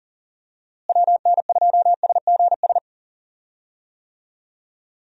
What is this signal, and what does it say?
Morse code, heard as a keyed tone, about 40 wpm, 700 Hz. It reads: WN2SGS
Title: USA Call Signs - 6 Characters - Repeated 3x - 40wpm